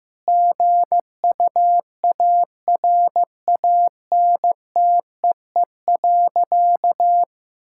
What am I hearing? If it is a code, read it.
Morse code, heard as a keyed tone, about 15 wpm, 700 Hz. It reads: GUARANTEE.
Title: Sentences from Top 1600 Words 15wpm